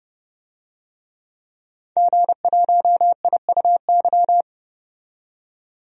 Morse code, heard as a keyed tone, about 32 wpm, 700 Hz. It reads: G1IUY
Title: DXCC-100 Call Signs - 5 Characters - 30wpm